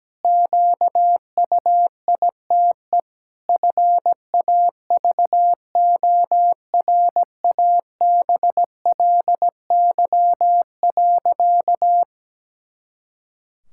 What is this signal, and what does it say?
Morse code, heard as a keyed tone, about 17 wpm, 700 Hz. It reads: QUITE FAVORABLY.
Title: Pangram Challenge v2 - 17wpm